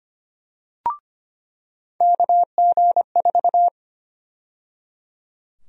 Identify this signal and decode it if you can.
Morse code, heard as a keyed tone, about 25 wpm, 700 Hz. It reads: KG4
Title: USA Call Sign Prefixes - 25wpm